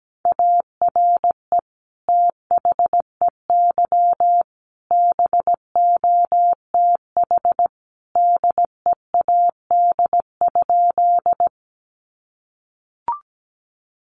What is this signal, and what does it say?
Morse code, heard as a keyed tone, about 17 wpm, 700 Hz. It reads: ARE THEY BOTH DEAD?
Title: Sentences from Top 1500 Words 17wpm